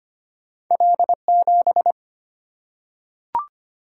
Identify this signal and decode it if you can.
Morse code, heard as a keyed tone, about 25 wpm, 690 Hz. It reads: L7
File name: Two Letter-Number Combinations - 25wpm